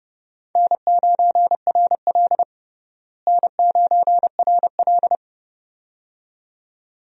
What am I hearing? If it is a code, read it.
Morse code, heard as a keyed tone, about 31 wpm, 700 Hz. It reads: N9RL N9RL
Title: USA Call Signs - 4 Characters - Repeated 2x - 30wpm